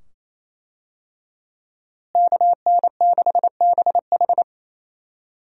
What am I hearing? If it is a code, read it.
Morse code, heard as a keyed tone, about 28 wpm, 700 Hz. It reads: KN6BH
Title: USA Call Signs - 5 Characters - Repeated 2x - 28wpm